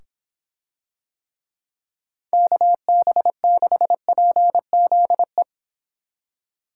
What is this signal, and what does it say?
Morse code, heard as a keyed tone, about 26 wpm, 700 Hz. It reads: KB6PZE